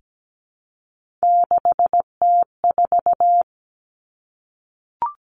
Morse code, heard as a keyed tone, about 17 wpm, 700 Hz. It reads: 6T4
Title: Three Challenging Letter-Number Combinations 17wpm